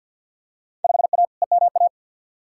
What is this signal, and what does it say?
Morse code, heard as a keyed tone, about 48 wpm, 700 Hz. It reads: 5A EMA
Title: Field Day Exchanges - 50wpm